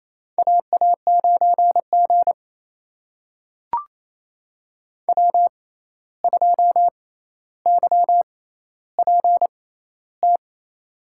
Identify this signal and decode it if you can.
Morse code, heard as a keyed tone, about 28 wpm, spaced as for 13 wpm, 700 Hz. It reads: AA9G W2YPT
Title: USA Call Signs - 6x Character Spacing - 28wpm